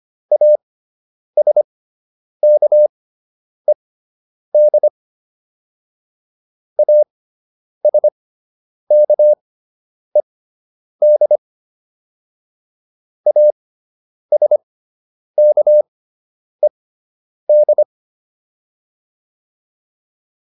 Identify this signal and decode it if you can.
Morse code, heard as a keyed tone, about 25 wpm, spaced as for 9 wpm, 600 Hz. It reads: ASKED ASKED ASKED
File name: ED5 09